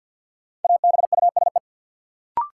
Morse code, heard as a keyed tone, about 53 wpm, 700 Hz. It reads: ABUSE
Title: Sentences from Top 1300 Words - New Word Drill - 50wpm